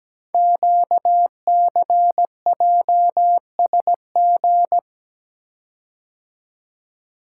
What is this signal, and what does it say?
Morse code, heard as a keyed tone, about 17 wpm, 700 Hz. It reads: QCJSG
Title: Code Groups - 5 Letters - 17wpm